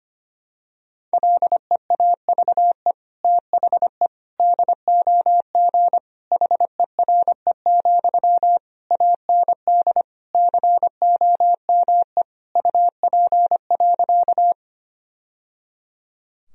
Morse code, heard as a keyed tone, about 25 wpm, 700 Hz. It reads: LEAVE THE DOG HERE, AND COME UP.
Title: Sentences from Top 1000 Words - Review Part 2 - 25wpm